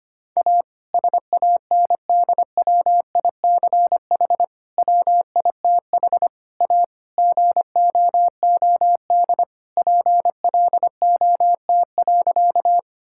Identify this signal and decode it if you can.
Morse code, heard as a keyed tone, about 25 wpm, 700 Hz. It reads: A SANDWICH WITH A GOOD PLOT.
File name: practice / 25wpm / All-Age Appropriate Jokes - 25wpm